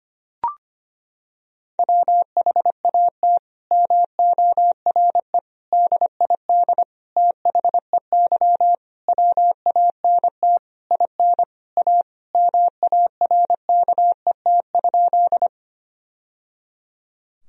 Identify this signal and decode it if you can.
Morse code, heard as a keyed tone, about 25 wpm, 700 Hz. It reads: WHAT MORE DID THEY WANT IN A MARKET?
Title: Sentences from Top 1000 Words - Review Part 3 - 25wpm